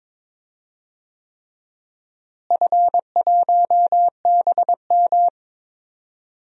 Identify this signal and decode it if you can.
Morse code, heard as a keyed tone, about 21 wpm, 700 Hz. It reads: F1BM
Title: DXCC-100 Call Signs - 4 Characters - Repeated 3x - 22wpm